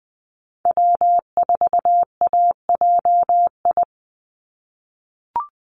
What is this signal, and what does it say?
Morse code, heard as a keyed tone, about 20 wpm, 700 Hz. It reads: W4AJI
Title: USA Call Signs 20wpm